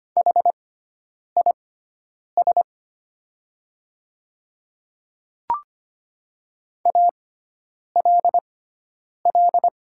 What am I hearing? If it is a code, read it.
Morse code, heard as a keyed tone, about 25 wpm, spaced as for 9 wpm, 700 Hz. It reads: HIS ALL
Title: 6x Character Spacing - Top 100 Words - 25wpm